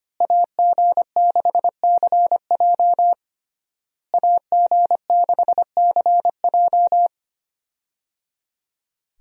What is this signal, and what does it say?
Morse code, heard as a keyed tone, about 25 wpm, 700 Hz. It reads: AG6CJ AG6CJ
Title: USA Call Signs - 5 Characters - Repeated 3x - 25wpm